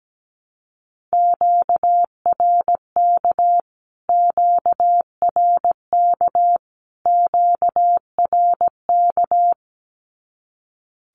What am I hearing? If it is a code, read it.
Morse code, heard as a keyed tone, about 17 wpm, 700 Hz. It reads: QRK QRK QRK
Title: Q-Codes Repeated 3x 17wpm